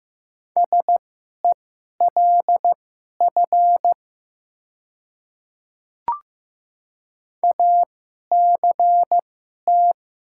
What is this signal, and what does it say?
Morse code, heard as a keyed tone, about 15 wpm, spaced as for 11 wpm, 700 Hz. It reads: SELF ACT
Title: 2x Character Spacing - Top 200 Words - 15wpm